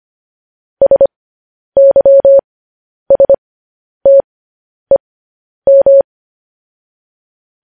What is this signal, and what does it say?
Morse code, heard as a keyed tone, about 25 wpm, spaced as for 10 wpm, 550 Hz. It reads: SYSTEM
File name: podcast / News headlines-1x-25x10